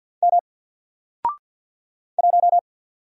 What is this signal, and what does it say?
Morse code, heard as a keyed tone, about 50 wpm, 700 Hz. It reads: M 1
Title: Single Letter-Number 50wpm